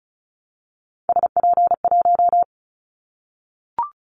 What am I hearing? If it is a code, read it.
Morse code, heard as a keyed tone, about 36 wpm, 700 Hz. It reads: SP1
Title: Callsign Prefixes for Top 25 DXCC - 35wpm